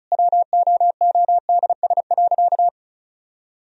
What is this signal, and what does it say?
Morse code, heard as a keyed tone, about 36 wpm, 700 Hz. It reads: WOODS.
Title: Sentences from Top 1000 Words - Review Part 2 - 35wpm